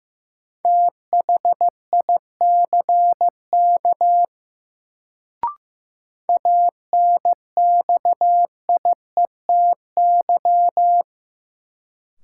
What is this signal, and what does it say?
Morse code, heard as a keyed tone, about 15 wpm, 700 Hz. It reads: THICK ANXIETY
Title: Sentences from Top 1600 Words - New Word Drill Alt - 15wpm